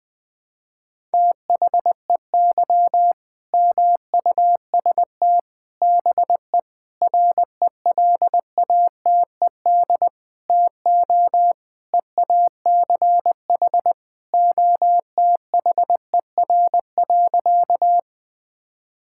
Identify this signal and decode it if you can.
Morse code, heard as a keyed tone, about 20 wpm, 700 Hz. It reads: THEY MUST BE RELATED TO EACH OTHER.
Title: Sentences from Top 1000 Words - Review Part 5 - 20wpm